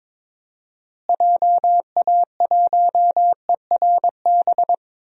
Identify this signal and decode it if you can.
Morse code, heard as a keyed tone, about 22 wpm, 700 Hz. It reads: JA1ERB
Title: DXCC-100 Call Signs - 6 Characters - Repeated 3x - 22wpm